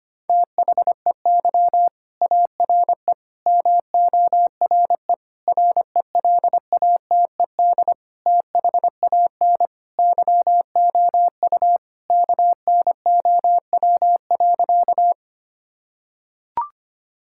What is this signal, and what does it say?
Morse code, heard as a keyed tone, about 25 wpm, 700 Hz. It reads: THEY ARE MORE RELATED THAN YOU KNOW.
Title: Sentences from Top 1000 Words - Review Part 5 - 25wpm